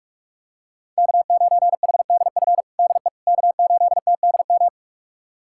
Morse code, heard as a keyed tone, about 42 wpm, 700 Hz. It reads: K9HDF DE K8TDM
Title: USA Callsign DE Callsign - 5 Characters - 45wpm